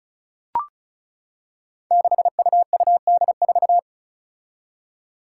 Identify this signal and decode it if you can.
Morse code, heard as a keyed tone, about 36 wpm, 700 Hz. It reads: 6UUD4
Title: Five Challenging Letter-Number Combinations 35wpm